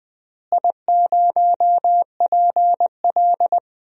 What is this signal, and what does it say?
Morse code, heard as a keyed tone, about 20 wpm, 700 Hz. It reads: I0PL
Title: DXCC-100 Call Signs - 4 Characters - Repeated 2x - 20wpm